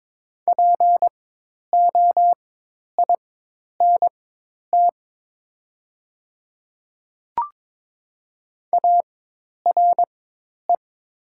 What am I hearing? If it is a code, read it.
Morse code, heard as a keyed tone, about 22 wpm, spaced as for 10 wpm, 700 Hz. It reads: POINT ARE
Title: 4x Character Spacing - Top 200 Words - 22wpm